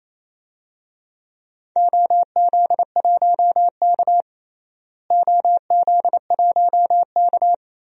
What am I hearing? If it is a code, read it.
Morse code, heard as a keyed tone, about 28 wpm, 700 Hz. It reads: OZ1K OZ1K
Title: DXCC-100 Call Signs - 4 Characters - Repeated 2x - 28wpm